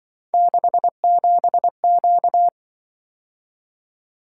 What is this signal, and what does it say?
Morse code, heard as a keyed tone, about 24 wpm, 700 Hz. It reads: 67Q
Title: Speed-Racing - Three Letter-Number Combinations - 20wpm